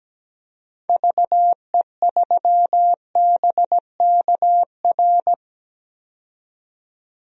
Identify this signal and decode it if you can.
Morse code, heard as a keyed tone, about 17 wpm, 700 Hz. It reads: VE3BKR
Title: DXCC-100 Call Signs - 6 Characters - 17wpm